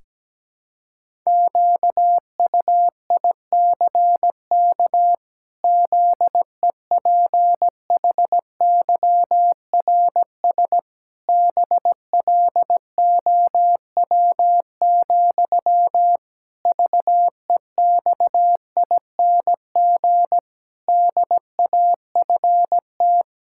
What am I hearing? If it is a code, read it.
Morse code, heard as a keyed tone, about 17 wpm, 700 Hz. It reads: QUICK ZEPHYRS BLOW, VEXING DAFT
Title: Pangram Challenge v2 - 17wpm